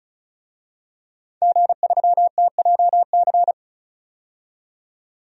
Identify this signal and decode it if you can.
Morse code, heard as a keyed tone, about 36 wpm, 700 Hz. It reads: G3TJC
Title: DXCC-100 Call Signs - 5 Characters - 35wpm